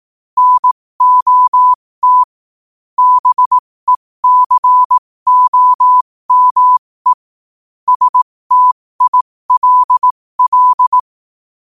Morse code, heard as a keyed tone, about 18 wpm, spaced as for 15 wpm, 1000 Hz. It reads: NOT BECOME STILL
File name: qotdfiles / 15wpm